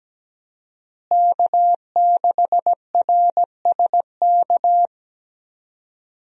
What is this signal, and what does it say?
Morse code, heard as a keyed tone, about 17 wpm, 700 Hz. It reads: K6RSK
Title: USA Call Signs - 5 Characters - Repeated 2x - 17wpm